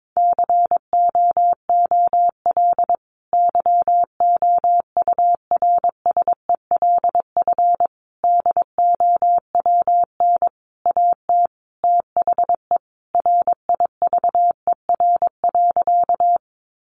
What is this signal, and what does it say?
Morse code, heard as a keyed tone, about 22 wpm, 700 Hz. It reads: COOL YOURSELF DOWN AT THE RIVER.
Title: Sentences from Top 1200 Words 22wpm